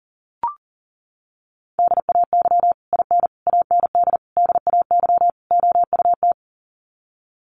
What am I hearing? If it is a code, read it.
Morse code, heard as a keyed tone, about 40 wpm, 700 Hz. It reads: DAY IN AND DAY OUT
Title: Sets of 5 Words Encore2 40wpm